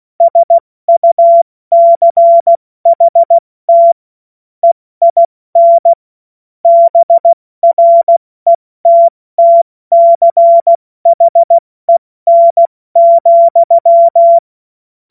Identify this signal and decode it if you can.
Morse code, heard as a keyed tone, about 16 wpm, 680 Hz. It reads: SUCHT EIN BRETTCHEN,
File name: Morsekurs / Witze / Witz 20. Januar 2026 70 F80